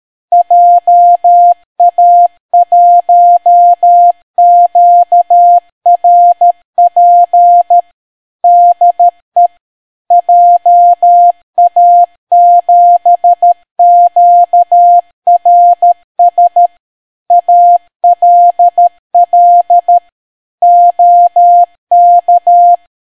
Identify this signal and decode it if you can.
Morse code, heard as a keyed tone, about 13 wpm, 700 Hz. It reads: JA1QRP DE JA7QRS ALL OK